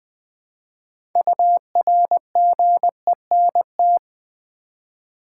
Morse code, heard as a keyed tone, about 20 wpm, 700 Hz. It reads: URGENT
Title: 6 Letter Words - First 3 Letters Given - 20wpm